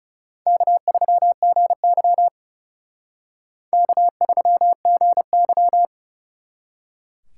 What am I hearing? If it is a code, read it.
Morse code, heard as a keyed tone, about 37 wpm, 700 Hz. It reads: K3GY K3GY
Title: Speed Racing - USA Call Signs 30wpm